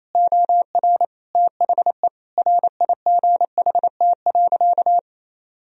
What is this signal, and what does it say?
Morse code, heard as a keyed tone, about 28 wpm, 700 Hz. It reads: OR THE RIGHT.
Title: Sentences from Top 1000 Words - Review Part 4 - 28wpm